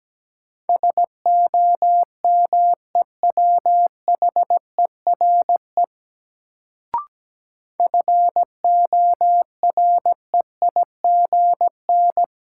Morse code, heard as a keyed tone, about 17 wpm, 700 Hz. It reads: SOMEWHERE FOREIGN
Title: Sentences from Top 1400 Words - New Word Drill - 17wpm